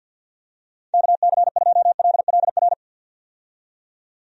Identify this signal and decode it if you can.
Morse code, heard as a keyed tone, about 49 wpm, 700 Hz. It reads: KC2LLR
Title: USA Call Signs - 6 Characters - 50wpm